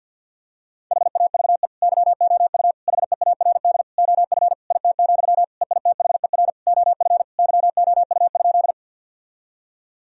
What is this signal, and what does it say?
Morse code, heard as a keyed tone, about 50 wpm, 700 Hz. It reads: HAVE YOU HEARD OF IT, EITHER OF YOU?
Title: Sentences from Top 1000 Words - Review Part 4 - 50wpm